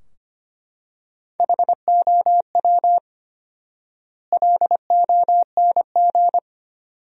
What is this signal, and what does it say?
Morse code, heard as a keyed tone, about 25 wpm, 700 Hz. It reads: HOW LONG
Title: 4x Word-Spacing - Sentences from Top 100 Words - 25wpm